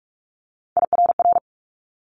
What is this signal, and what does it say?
Morse code, heard as a keyed tone, about 46 wpm, 700 Hz. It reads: IRR
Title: Multi-letter Phonemes Quick Fire 45wpm